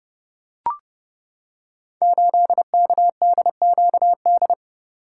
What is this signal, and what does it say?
Morse code, heard as a keyed tone, about 30 wpm, 700 Hz. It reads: 8KDQD
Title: Code Groups - 5 Letter:Numbers - 30wpm